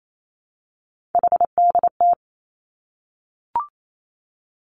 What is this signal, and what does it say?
Morse code, heard as a keyed tone, about 28 wpm, 700 Hz. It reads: HDT
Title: Three Challenging Letter-Number Combinations 28wpm